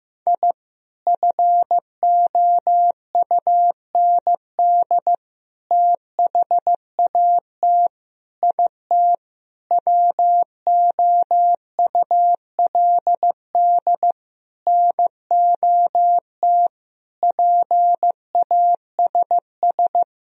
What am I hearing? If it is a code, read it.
Morse code, heard as a keyed tone, about 15 wpm, 700 Hz. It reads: I FOUND THAT IT WOULD NOT PASS
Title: Sentences from Top 1000 Words - Review Part 1 - 15wpm